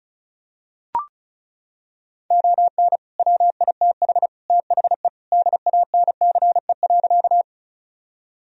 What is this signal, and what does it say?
Morse code, heard as a keyed tone, about 34 wpm, 700 Hz. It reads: ON WITH THE DANCE.
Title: Sentences from Top 1000 Words - Review Part 4 - 35wpm